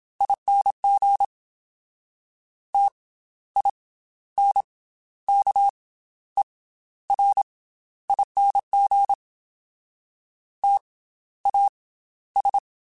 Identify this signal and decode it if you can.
Morse code, heard as a keyed tone, about 27 wpm, spaced as for 13 wpm, 800 Hz. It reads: ING TINKERING TAS
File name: Special INGs / Special INGs-mp3s / Slow ing @25